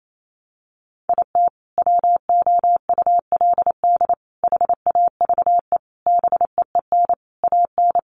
Morse code, heard as a keyed tone, about 28 wpm, 700 Hz. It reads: IT WOULD HAVE BEEN AN
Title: Sentences from Top 1100 Words 28wpm